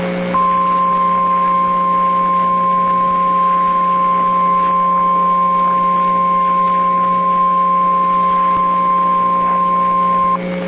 Emergency Alert System (SAME) tones (the 1050 Hz attention tone), with noise. It is heard from 0.3 to 10.4 s.